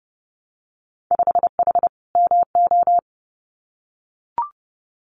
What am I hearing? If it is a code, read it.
Morse code, heard as a keyed tone, about 30 wpm, 700 Hz. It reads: 5H MO
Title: Winter Field Day Exchanges Encore 30wpm